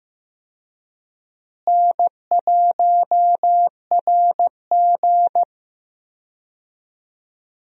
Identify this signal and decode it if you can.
Morse code, heard as a keyed tone, about 15 wpm, 700 Hz. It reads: N1RG
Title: USA Call Signs - 4 Characters - Repeated 2x - 15wpm